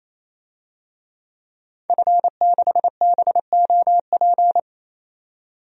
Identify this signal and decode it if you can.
Morse code, heard as a keyed tone, about 28 wpm, 700 Hz. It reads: F6BOP
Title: DXCC-100 Call Signs - 5 Characters - 28wpm